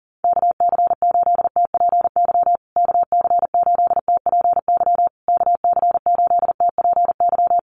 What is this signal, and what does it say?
Morse code, heard as a keyed tone, about 40 wpm, 700 Hz. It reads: KC8TPY KC8TPY KC8TPY
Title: USA Call Signs Repeated 3x 40wpm